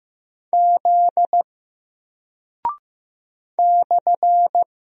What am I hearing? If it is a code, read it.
Morse code, heard as a keyed tone, about 15 wpm, 700 Hz. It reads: Z /
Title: Single Character (Letter:Number:Punctuation:Prosign) Alt 15wpm